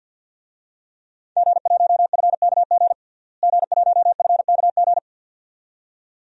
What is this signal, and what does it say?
Morse code, heard as a keyed tone, about 51 wpm, 700 Hz. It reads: G1FKG G1FKG
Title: DXCC-100 Call Signs - 5 Characters - Repeated 2x - 50wpm